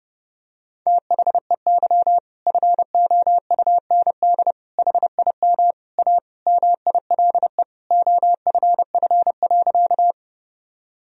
Morse code, heard as a keyed tone, about 32 wpm, 700 Hz. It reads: THEY FOUND HIM A MILE OFF.
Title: Sentences from Top 1000 Words - Review Part 1 - 30wpm